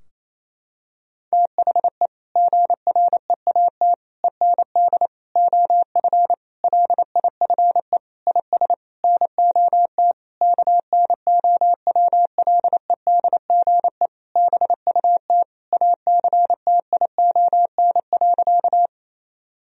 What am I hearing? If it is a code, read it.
Morse code, heard as a keyed tone, about 27 wpm, 700 Hz. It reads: THE GREAT END OF LIFE IS NOT KNOWLEDGE BUT ACTION.
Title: Sentences from Top 1000 Words - Review Part 4 - 28wpm